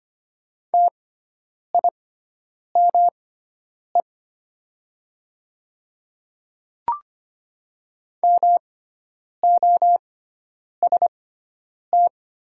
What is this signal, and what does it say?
Morse code, heard as a keyed tone, about 25 wpm, spaced as for 9 wpm, 700 Hz. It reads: TIME MOST
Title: 6x Character Spacing - Top 100 Words - 25wpm